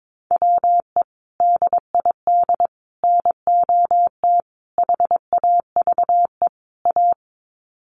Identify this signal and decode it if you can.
Morse code, heard as a keyed tone, about 22 wpm, 700 Hz. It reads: WE DID NOT HAVE A
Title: Sets of 5 Words Encore2 22wpm